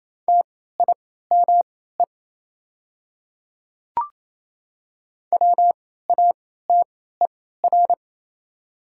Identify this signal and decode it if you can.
Morse code, heard as a keyed tone, about 27 wpm, spaced as for 16 wpm, 700 Hz. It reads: TIME WATER
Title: 3x Character Spacing - Top 100 Words - 28wpm